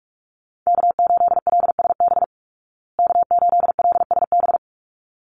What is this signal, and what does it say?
Morse code, heard as a keyed tone, about 44 wpm, 700 Hz. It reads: K8LSB K8LSB
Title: Call Signs Repeated 3x - 45wpm